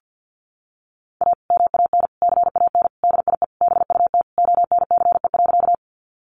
Morse code, heard as a keyed tone, about 48 wpm, 700 Hz. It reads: A MAN CAN DIE BUT ONCE.
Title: Sentences from Top 1300 Words 50wpm